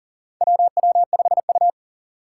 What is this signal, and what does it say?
Morse code, heard as a keyed tone, about 40 wpm, 700 Hz. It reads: WW5U
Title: USA Call Signs - 2x Character Spacing - 40wpm